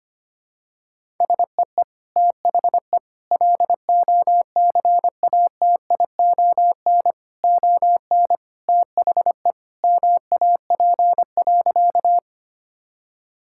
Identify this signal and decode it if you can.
Morse code, heard as a keyed tone, about 25 wpm, 700 Hz. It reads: SEE THE LOCATION ON THE MAP.
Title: Sentences from Top 1000 Words - Review Part 6 - 25wpm